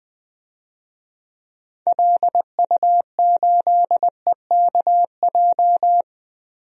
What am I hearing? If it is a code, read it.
Morse code, heard as a keyed tone, about 20 wpm, 700 Hz. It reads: LU8EKJ
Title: DXCC-100 Call Signs - 6 Characters - Repeated 3x - 20wpm